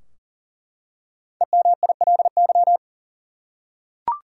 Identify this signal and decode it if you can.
Morse code, heard as a keyed tone, about 40 wpm, 700 Hz. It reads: EMILY
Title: USA Common Names - Max 5 Letters - 40wpm